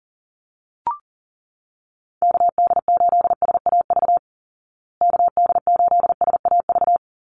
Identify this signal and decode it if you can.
Morse code, heard as a keyed tone, about 40 wpm, 700 Hz. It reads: KD8SAV KD8SAV
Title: Call Signs Repeated 2x - 40wpm